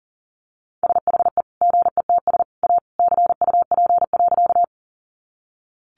Morse code, heard as a keyed tone, about 40 wpm, 700 Hz. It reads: SHE GETS A CUP.
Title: Additional Easy Phrases 40wpm